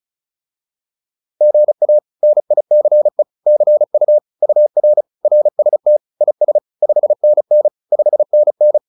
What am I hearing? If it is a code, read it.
Morse code, heard as a keyed tone, about 34 wpm, 600 Hz. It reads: GA NICE CU UR RST IS 5NN 5NN